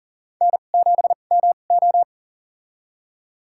Morse code, heard as a keyed tone, about 40 wpm, spaced as for 29 wpm, 700 Hz. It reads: N7MO
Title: USA Call Signs - 4 Characters - 2x Character Spacing - 40wpm